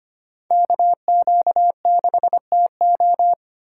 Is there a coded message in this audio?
Yes: Morse code from 0.5 to 3.3 s, heard as a keyed tone, about 32 wpm, 700 Hz. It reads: KQ6TO